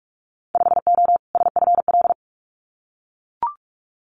Morse code, heard as a keyed tone, about 44 wpm, 700 Hz. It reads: 5O SFL
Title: Winter Field Day Exchanges 45wpm